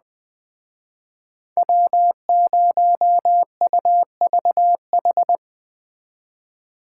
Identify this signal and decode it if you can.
Morse code, heard as a keyed tone, about 20 wpm, 700 Hz. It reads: W0UVH
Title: Speed Racing - USA 1x3 Call Signs 17wpm